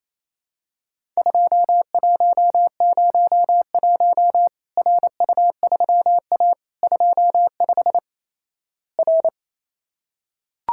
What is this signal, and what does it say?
Morse code, heard as a keyed tone, about 28 wpm, 700 Hz. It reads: 2101 RU3A 25 R
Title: Worked All Europe DX Contest - QTC Exchange - 28wpm